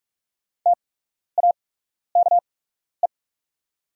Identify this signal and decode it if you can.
Morse code, heard as a keyed tone, about 44 wpm, spaced as for 12 wpm, 690 Hz. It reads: TAKE
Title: 8x Character Spacing - Top 200 Words - 45wpm